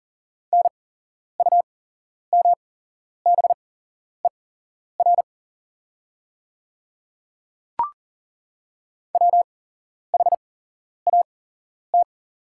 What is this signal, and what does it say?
Morse code, heard as a keyed tone, about 40 wpm, spaced as for 11 wpm, 700 Hz. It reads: NUMBER WHAT
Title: 8x Character Spacing - Top 100 Words - 40wpm